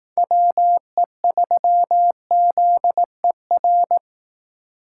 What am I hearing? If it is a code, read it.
Morse code, heard as a keyed tone, about 18 wpm, 700 Hz. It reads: WE3ZER
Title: Speed Racing - USA 2x3 Call Signs 15wpm